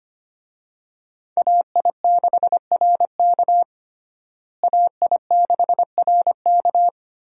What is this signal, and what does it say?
Morse code, heard as a keyed tone, about 25 wpm, 700 Hz. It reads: AI6RK AI6RK
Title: USA Call Signs - 5 Characters - Repeated 3x - 25wpm